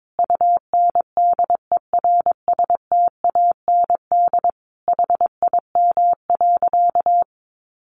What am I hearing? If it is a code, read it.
Morse code, heard as a keyed tone, about 22 wpm, 700 Hz. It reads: UNDERSTAND HIM.
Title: Sentences from Top 600 Words 22wpm